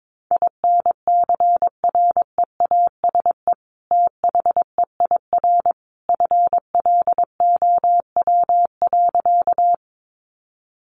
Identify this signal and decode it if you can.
Morse code, heard as a keyed tone, about 22 wpm, 700 Hz. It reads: INCREASE THEIR FLOW.
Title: Sentences from Top 1200 Words 22wpm